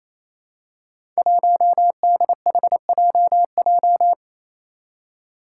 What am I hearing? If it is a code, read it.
Morse code, heard as a keyed tone, about 29 wpm, 700 Hz. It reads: 1DHJJ
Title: Five Challenging Letter-Number Combinations 28wpm